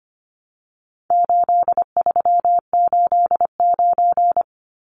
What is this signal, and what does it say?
Morse code, heard as a keyed tone, about 25 wpm, 700 Hz. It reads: 8389
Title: Numbers - 4-Digit 25wpm